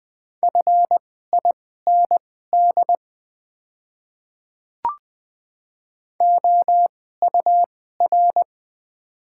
Morse code, heard as a keyed tone, about 20 wpm, spaced as for 14 wpm, 700 Hz. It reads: FIND OUR
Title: 2x Character Spacing - Top 300 Words - 20wpm